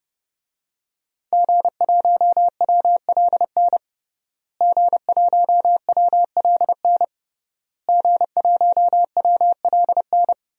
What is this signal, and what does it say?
Morse code, heard as a keyed tone, about 30 wpm, 700 Hz. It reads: G1WLN G1WLN G1WLN
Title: DXCC-100 Call Signs - 5 Characters - Repeated 3x - 30wpm